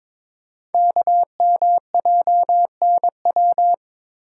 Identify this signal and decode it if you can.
Morse code, heard as a keyed tone, about 22 wpm, 700 Hz. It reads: KMJNW
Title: Code Groups - 5 Letters - 22wpm